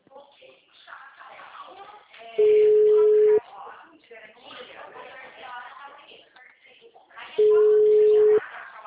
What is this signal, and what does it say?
Ringback tone (Europe, 425 Hz), with speech.
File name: out- -2001- - - .36594